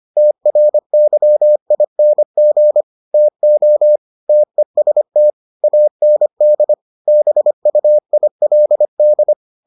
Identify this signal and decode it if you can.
Morse code, heard as a keyed tone, about 25 wpm, 600 Hz. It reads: TRYING TO TEST AND BUILD